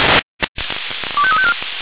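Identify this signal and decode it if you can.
Five-tone selective call (ZVEI), heard as quick runs of five tones, tones of about 70 ms. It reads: 25354
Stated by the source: ZVEI five-tone call received by SDR.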